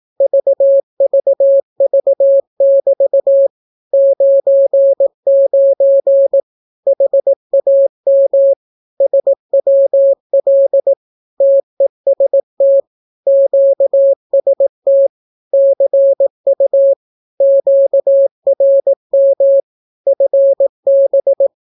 Morse code, heard as a keyed tone, about 18 wpm, 550 Hz. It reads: VVV<BT> 99 HAM SWL TEST QST CU QRM FB
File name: naukapodstaw / skroty / skroty